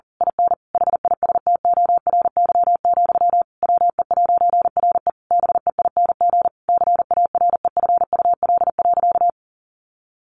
Morse code, heard as a keyed tone, about 40 wpm, 700 Hz. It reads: IN HISTORY, WE'RE BEING CAREFUL.
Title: A Sound Of Thunder 40wpm